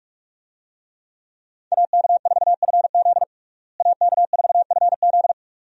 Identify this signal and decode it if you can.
Morse code, heard as a keyed tone, about 46 wpm, 700 Hz. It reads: AK4FZ AK4FZ
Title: USA Call Signs - 5 Characters - Repeated 2x - 45wpm